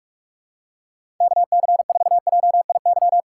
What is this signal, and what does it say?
Morse code, heard as a keyed tone, about 43 wpm, 700 Hz. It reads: KC4JIY
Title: USA Call Signs - 6 Characters - 2x Character Spacing - 45wpm